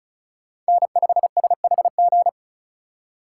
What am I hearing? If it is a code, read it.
Morse code, heard as a keyed tone, about 36 wpm, 700 Hz. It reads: N5SHG
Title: USA Call Signs - 5 Characters - 6x Character Spacing - 35wpm